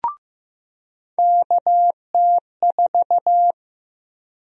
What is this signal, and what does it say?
Morse code, heard as a keyed tone, about 16 wpm, 700 Hz. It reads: KT4
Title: USA Call Sign Prefixes - 15wpm